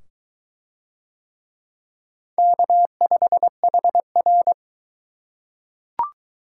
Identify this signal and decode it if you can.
Morse code, heard as a keyed tone, about 20 wpm, 700 Hz. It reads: K5HR